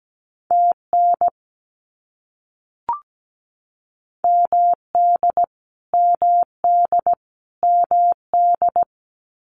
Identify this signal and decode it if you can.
Morse code, heard as a keyed tone, about 17 wpm, 700 Hz. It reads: TN MD MD MD
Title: US States Repeated 3x 17wpm